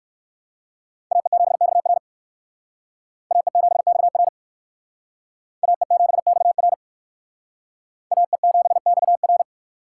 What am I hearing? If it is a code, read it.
Morse code, heard as a keyed tone, about 56 wpm, 700 Hz. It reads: AE7XR AE7XR AE7XR AE7XR